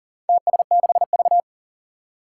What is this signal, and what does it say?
Morse code, heard as a keyed tone, about 40 wpm, 700 Hz. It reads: TS6V
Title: Four Challenging Letter-Number Combinations 40wpm